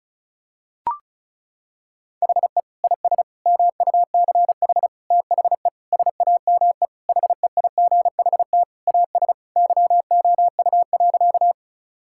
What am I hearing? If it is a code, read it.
Morse code, heard as a keyed tone, about 36 wpm, 700 Hz. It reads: HE IS MUCH THE SAME HEIGHT AS YOU.
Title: Sentences from Top 1600 Words 35wpm